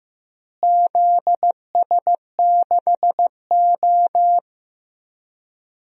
Morse code, heard as a keyed tone, about 15 wpm, 700 Hz. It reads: ZS6O
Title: DXCC-100 Call Signs - 4 Characters - Repeated 3x - 15wpm